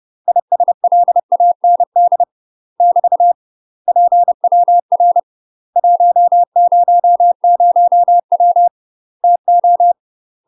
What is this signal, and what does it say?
Morse code, heard as a keyed tone, about 30 wpm, 700 Hz. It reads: ISLAND <BT> PWR 100W TO
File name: UK MASTER QSOs TRAINING / 30WPM / UK MASTER QSOs 0020